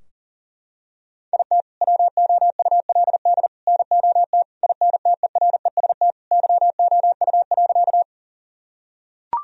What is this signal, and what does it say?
Morse code, heard as a keyed tone, about 40 wpm, 700 Hz. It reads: IT WOULD NOT INTEREST YOU.
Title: Sentences from Top 1000 Words - Review Part 2 - 40wpm